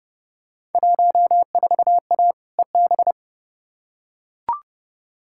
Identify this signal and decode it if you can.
Morse code, heard as a keyed tone, about 30 wpm, 700 Hz. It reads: 14A EB
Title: Field Day Exchanges Encore - 30wpm